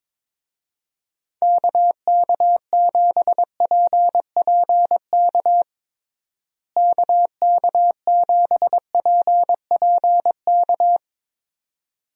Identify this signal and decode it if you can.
Morse code, heard as a keyed tone, about 22 wpm, 700 Hz. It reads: KK7PPK KK7PPK
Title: USA Call Signs - 6 Characters - Repeated 2x - 22wpm